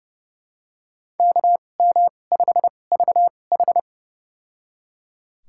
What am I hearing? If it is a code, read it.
Morse code, heard as a keyed tone, about 30 wpm, spaced as for 22 wpm, 700 Hz. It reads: KM5VH
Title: USA Call Signs - 5 Characters - 2x Character Spacing - 30wpm